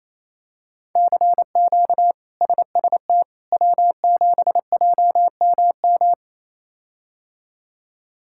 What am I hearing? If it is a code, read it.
Morse code, heard as a keyed tone, about 28 wpm, 700 Hz. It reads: CQ SST W7JMM
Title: K1USN SST QSO Practice - 28wpm